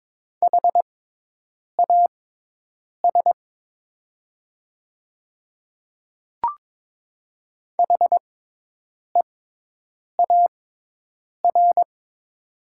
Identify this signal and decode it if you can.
Morse code, heard as a keyed tone, about 22 wpm, spaced as for 8 wpm, 700 Hz. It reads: HAS HEAR
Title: 6x Character Spacing - Top 300 Words - 22wpm